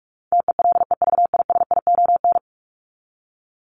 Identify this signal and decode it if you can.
Morse code, heard as a keyed tone, about 44 wpm, 700 Hz. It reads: TELEVISION
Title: Top 501-750 Words 45wpm